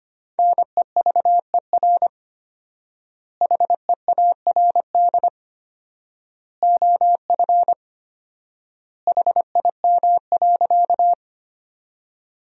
Morse code, heard as a keyed tone, about 25 wpm, 700 Hz. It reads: NEVER HEARD OF HIM.
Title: 4x Word-Spacing - Sentences from Top 400 Words - 25wpm